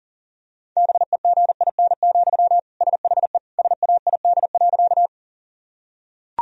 Morse code, heard as a keyed tone, about 37 wpm, 700 Hz. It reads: BEGIN, SHE SAID.
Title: Sentences from Top 1000 Words - Review Part 1 - 40wpm